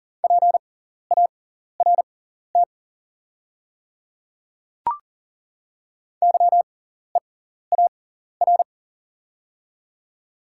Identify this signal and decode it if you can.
Morse code, heard as a keyed tone, about 40 wpm, spaced as for 14 wpm, 690 Hz. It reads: PART YEAR
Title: 6x Character Spacing - Top 200 Words - 40wpm